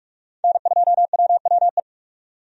Morse code, heard as a keyed tone, about 45 wpm, 700 Hz. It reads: N2WWE
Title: USA Call Signs - 5 Characters - 45wpm